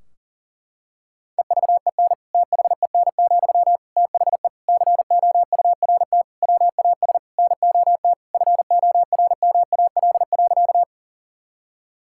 Morse code, heard as a keyed tone, about 40 wpm, 700 Hz. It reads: EVEN THEN, THE COURT WAS NOT FORMAL.
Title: Sentences from Top 1600 Words 40wpm